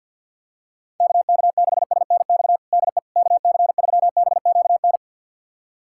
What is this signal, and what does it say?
Morse code, heard as a keyed tone, about 49 wpm, 700 Hz. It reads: KK6SNX DE KC3BQN
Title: USA Callsign DE Callsign - 6 Characters - 50wpm